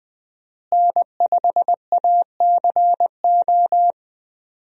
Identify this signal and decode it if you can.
Morse code, heard as a keyed tone, about 20 wpm, 700 Hz. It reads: N5ACO
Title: USA Call Signs - 5 Characters - Repeated 3x - 20wpm